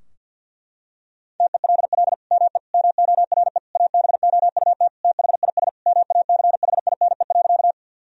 Morse code, heard as a keyed tone, about 52 wpm, 700 Hz. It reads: TELL ME MORE ABOUT THIS MACHINE.
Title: Sentences from Top 1000 Words - Review Part 2 - 50wpm